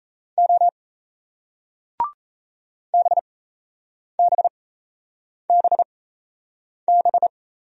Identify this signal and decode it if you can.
Morse code, heard as a keyed tone, about 39 wpm, spaced as for 8 wpm, 700 Hz. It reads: O BBBB